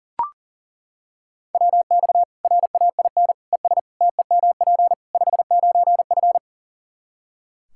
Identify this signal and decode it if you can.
Morse code, heard as a keyed tone, about 37 wpm, 700 Hz. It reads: WX RAIN ES TEMP 59F
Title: Practice QSO - Exchange 2 - Info - 40wpm